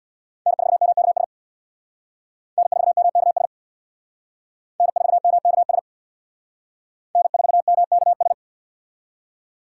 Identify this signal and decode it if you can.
Morse code, heard as a keyed tone, about 61 wpm, 700 Hz. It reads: N4MKS N4MKS N4MKS N4MKS
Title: Speed Racing - USA 1x3 Call Signs 50wpm